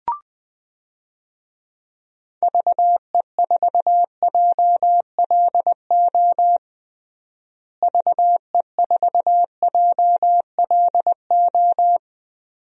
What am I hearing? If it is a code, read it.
Morse code, heard as a keyed tone, about 20 wpm, 700 Hz. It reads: VE4JLO VE4JLO
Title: DXCC-100 Call Signs - 6 Characters - Repeated 2x - 20wpm